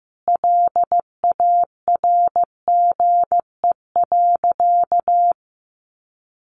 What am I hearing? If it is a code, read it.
Morse code, heard as a keyed tone, about 15 wpm, 700 Hz. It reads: LARGE.
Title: Sentences from Top 700 Words 15wpm